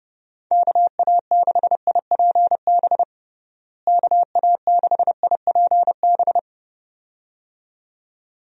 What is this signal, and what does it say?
Morse code, heard as a keyed tone, about 32 wpm, 700 Hz. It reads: KA6IPB KA6IPB
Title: USA Call Signs - 6 Characters - Repeated 2x - 30wpm